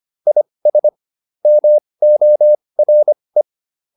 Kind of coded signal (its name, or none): Morse code